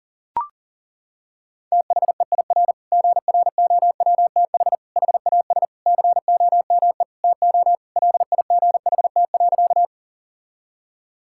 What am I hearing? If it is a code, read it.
Morse code, heard as a keyed tone, about 37 wpm, 700 Hz. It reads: THEIR GROWTH HAS COME TO LIGHT.
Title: Sentences from Top 1000 Words - Review Part 5 - 40wpm